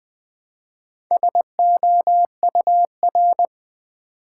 Morse code, heard as a keyed tone, about 20 wpm, 700 Hz. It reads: SOUR
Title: Build A Word - 2 Letters At A Time - 6-Letter Words - 20wpm